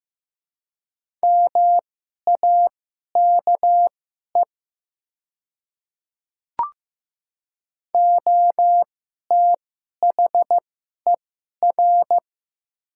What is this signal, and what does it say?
Morse code, heard as a keyed tone, about 15 wpm, spaced as for 11 wpm, 700 Hz. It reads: MAKE OTHER
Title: 2x Character Spacing - Top 100 Words - 15wpm